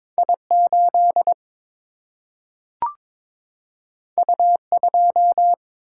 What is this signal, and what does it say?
Morse code, heard as a keyed tone, about 22 wpm, 700 Hz. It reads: I 8 U 2
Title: Single-Letter Single-Number 22wpm